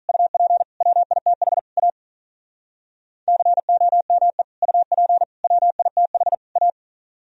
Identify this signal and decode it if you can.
Morse code, heard as a keyed tone, about 44 wpm, 700 Hz. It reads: UP WITH A COME UP WITH A
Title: Speed-Racing - Sets of 4 Words 35wpm